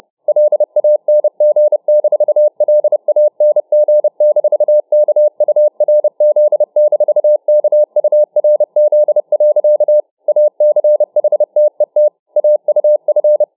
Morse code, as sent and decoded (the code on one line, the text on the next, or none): .-.. .- -. --. -....- .-.. .- -. --. -....- -.- ..- .-. --.. -....- -.- ..- .-. --.. .-.-.- / .- -.-. .... - . - / .- ..- ..-.
LANG-LANG-KURZ-KURZ. ACHTET AUF